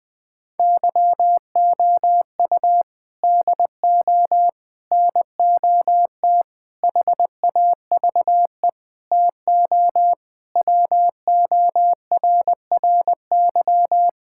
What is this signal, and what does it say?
Morse code, heard as a keyed tone, about 21 wpm, 700 Hz. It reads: YOU DO NOT HAVE TO WORRY
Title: Sentences with 5-Word Phrases - Spoken Beginning - 20wpm